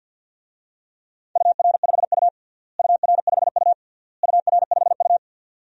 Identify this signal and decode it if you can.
Morse code, heard as a keyed tone, about 47 wpm, 700 Hz. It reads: UR5U UR5U UR5U
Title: DXCC-100 Call Signs - 4 Characters - Repeated 3x - 50wpm